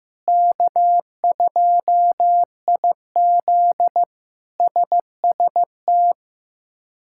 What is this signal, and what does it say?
Morse code, heard as a keyed tone, about 16 wpm, 700 Hz. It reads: K2IZ SST
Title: K1USN SST QSO Practice - 15wpm